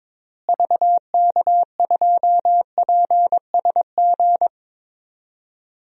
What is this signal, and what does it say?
Morse code, heard as a keyed tone, about 22 wpm, 700 Hz. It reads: VK2PSG
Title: DXCC-100 Call Signs - 6 Characters - Repeated 2x - 22wpm